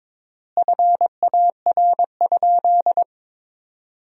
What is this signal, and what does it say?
Morse code, heard as a keyed tone, about 22 wpm, 700 Hz. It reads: FAR?
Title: 3x Word-Spacing - Sentences from Top 300 Words - 22wpm